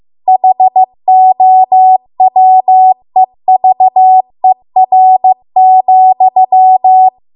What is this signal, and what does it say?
Morse code, heard as a keyed tone, about 15 wpm, 750 Hz. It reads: HOWEVER,
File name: Archive / 15 WPM / 15WPM